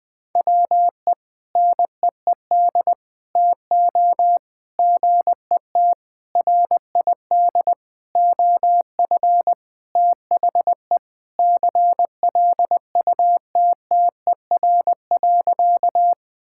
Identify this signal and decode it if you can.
Morse code, heard as a keyed tone, about 20 wpm, 700 Hz. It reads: WE NEED TO GET RID OF THE CLUTTER.
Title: Sentences with 5-Word Phrases - Spoken Beginning - 20wpm